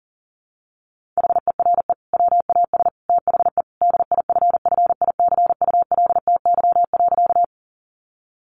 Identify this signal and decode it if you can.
Morse code, heard as a keyed tone, about 40 wpm, 700 Hz. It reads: HERE WAS THE DIFFICULTY.
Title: Sentences from Top 1500 Words 40wpm